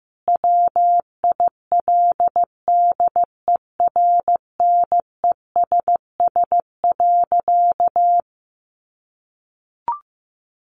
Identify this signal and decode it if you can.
Morse code, heard as a keyed tone, about 15 wpm, 700 Hz. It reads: WILDERNESS.
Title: A Sound Of Thunder 15wpm